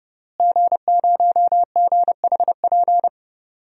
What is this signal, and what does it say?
Morse code, heard as a keyed tone, about 30 wpm, 700 Hz. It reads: G0GHP
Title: DXCC-100 Call Signs - 5 Characters - Repeated 2x - 30wpm